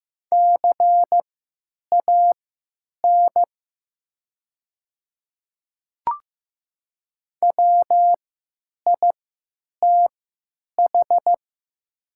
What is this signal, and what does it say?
Morse code, heard as a keyed tone, about 15 wpm, spaced as for 9 wpm, 700 Hz. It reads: CAN WITH
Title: 3x Character Spacing - Top 100 Words - 15wpm